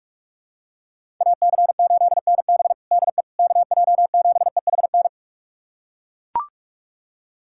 Morse code, heard as a keyed tone, about 42 wpm, 700 Hz. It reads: AC8NB DE KJ7EHN
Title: USA Callsign DE Callsign 45wpm